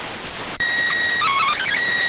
SSTV (slow-scan television), PD 290.
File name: findings / CT 205 ax mips, iDose 3 0004.gen-32 rx